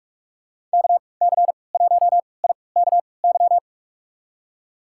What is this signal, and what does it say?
Morse code, heard as a keyed tone, about 46 wpm, spaced as for 25 wpm, 700 Hz. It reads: KC1IKY
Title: USA Call Signs - 3x Character Spacing - 45wpm